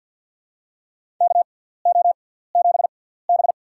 Morse code, heard as a keyed tone, about 54 wpm, spaced as for 17 wpm, 700 Hz. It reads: KO7B
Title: USA Call Signs - 4 Characters - 6x Character Spacing - 50wpm